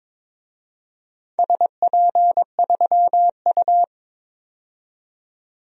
Morse code, heard as a keyed tone, about 22 wpm, 700 Hz. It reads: SP3U
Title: DXCC-100 Call Signs - 4 Characters - 22wpm